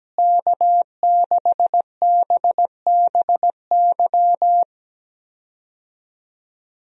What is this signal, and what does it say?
Morse code, heard as a keyed tone, about 17 wpm, 700 Hz. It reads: K6BBY
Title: USA Call Signs - 5 Characters - Repeated 2x - 17wpm